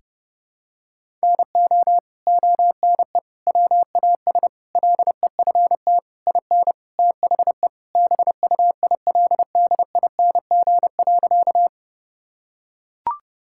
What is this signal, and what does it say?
Morse code, heard as a keyed tone, about 32 wpm, 700 Hz. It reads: NO ONE WAS LEFT IN THE BUILDING.
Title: Sentences from Top 1000 Words - Review Part 5 - 30wpm